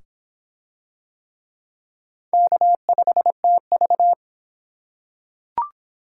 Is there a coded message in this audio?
Yes: Morse code from 2.3 to 4.1 s, heard as a keyed tone, about 22 wpm, 700 Hz. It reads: K5TV